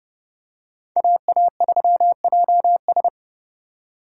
Morse code, heard as a keyed tone, about 30 wpm, 700 Hz. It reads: AA3JS
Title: USA Call Signs - 5 Characters - 4x Character Spacing - 30wpm